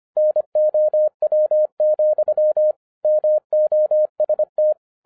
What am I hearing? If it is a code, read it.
Morse code, heard as a keyed tone, about 25 wpm, 610 Hz. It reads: NOW, MOST
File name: 2012-01-mp3 / n14-FUN WITH NUMBERS-25wpm